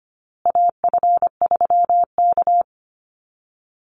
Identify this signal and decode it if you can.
Morse code, heard as a keyed tone, about 25 wpm, 700 Hz. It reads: AF3K
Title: Call Signs Repeated 2x - 25wpm